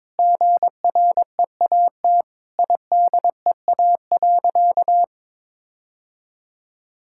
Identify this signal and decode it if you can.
Morse code, heard as a keyed tone, about 21 wpm, 700 Hz. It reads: GREAT IDEA.
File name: practice / 22wpm / Sentences with 3-Word Phrases - 22wpm